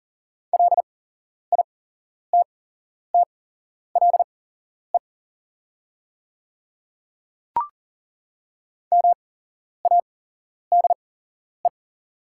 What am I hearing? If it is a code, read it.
Morse code, heard as a keyed tone, about 41 wpm, spaced as for 11 wpm, 700 Hz. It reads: LITTLE MADE
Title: 8x Character Spacing - Top 300 Words - 40wpm